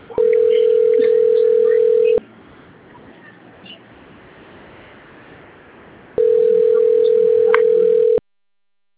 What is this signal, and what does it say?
Ringback tone (North America), 440 + 480 Hz.